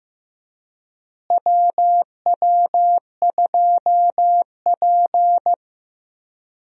Morse code, heard as a keyed tone, about 16 wpm, 700 Hz. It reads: WW2P
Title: USA Call Signs - 4 Characters - Repeated 2x - 15wpm